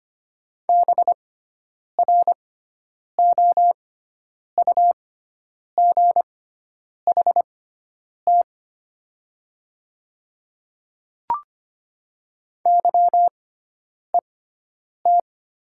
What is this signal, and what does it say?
Morse code, heard as a keyed tone, about 25 wpm, spaced as for 9 wpm, 700 Hz. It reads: BROUGHT YET
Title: 6x Character Spacing - Top 500 Words - 25wpm